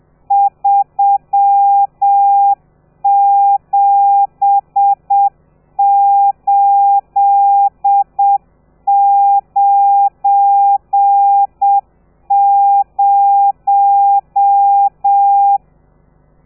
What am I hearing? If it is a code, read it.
Morse code, heard as a keyed tone, about 7 wpm, 800 Hz. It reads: 37890
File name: telegrafia / i2tfj / Lez12 30